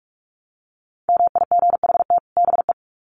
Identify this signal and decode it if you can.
Morse code, heard as a keyed tone, about 44 wpm, 700 Hz. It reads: MIGHT BE
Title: Sets of 2 Words 45wpm